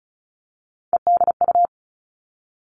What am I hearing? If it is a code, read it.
Morse code, heard as a keyed tone, about 36 wpm, 700 Hz. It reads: EDU
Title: Three Challenging Letter-Number Combinations 35wpm